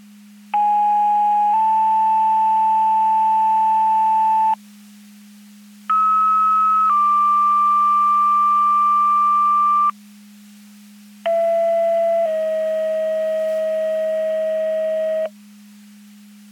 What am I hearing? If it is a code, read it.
Two-tone page, heard as a one-second tone, then a three-second tone, about 830 then 880 Hz, about 1290 then 1180 Hz, about 670 then 640 Hz. It is heard from 0.5 to 15.3 s.
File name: Portage 154.385 Paging